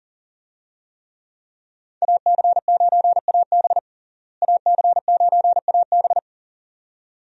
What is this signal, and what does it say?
Morse code, heard as a keyed tone, about 40 wpm, 700 Hz. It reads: AC9AB AC9AB
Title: USA Call Signs - 5 Characters - Repeated 2x - 40wpm